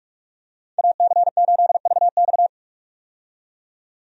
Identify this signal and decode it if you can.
Morse code, heard as a keyed tone, about 41 wpm, 700 Hz. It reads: AC8VX
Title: USA Call Signs - 5 Characters - 45wpm